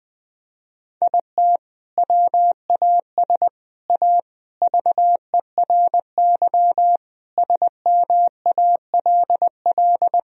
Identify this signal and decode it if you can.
Morse code, heard as a keyed tone, about 19 wpm, 700 Hz. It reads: IT WAS A VERY SMALL
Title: Sentences from Top 1000 Words - Review Part 1 - 20wpm